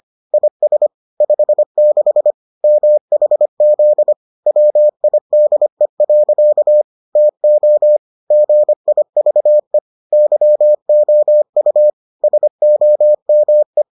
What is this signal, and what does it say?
Morse code, heard as a keyed tone, about 25 wpm, 600 Hz. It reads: IS 56 MHZ WIDE. TO GIVE YOU SOME